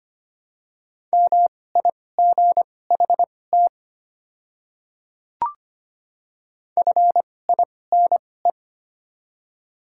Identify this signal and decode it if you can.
Morse code, heard as a keyed tone, about 25 wpm, spaced as for 18 wpm, 700 Hz. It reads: MIGHT FINE
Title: 2x Character Spacing - Top 500 Words - 25wpm